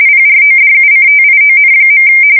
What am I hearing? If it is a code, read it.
RTTY, about 45.45 baud. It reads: PRODUCTION.
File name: fsk170 / teletype-corp